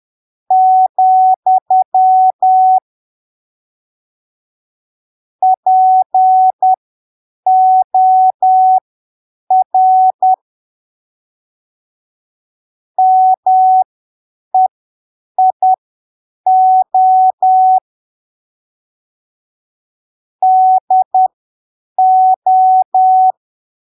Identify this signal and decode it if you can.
Morse code, heard as a keyed tone, about 10 wpm, spaced as for 7 wpm, 750 Hz. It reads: , POR MEIO DO
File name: arquivos / texto / texto 18 10ppm